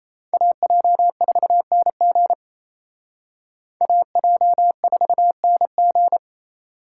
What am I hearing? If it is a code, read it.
Morse code, heard as a keyed tone, about 35 wpm, 700 Hz. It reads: AJ4NG AJ4NG